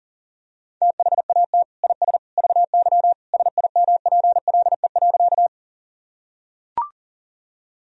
Morse code, heard as a keyed tone, about 40 wpm, 700 Hz. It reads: THAT IS VY SIMPLE.
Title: Sentences from Top 400 Words and Abbreviations 40wpm